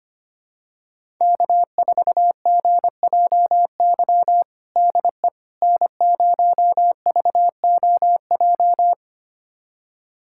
Morse code, heard as a keyed tone, about 25 wpm, 700 Hz. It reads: K4GJY DE N0VOJ
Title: USA Callsign DE Callsign - 5 Characters - 25wpm